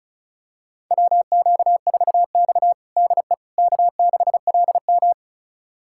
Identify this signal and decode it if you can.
Morse code, heard as a keyed tone, about 36 wpm, 700 Hz. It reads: WQ4X DE K6LM
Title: USA Callsign DE Callsign - 4 Characters - 35wpm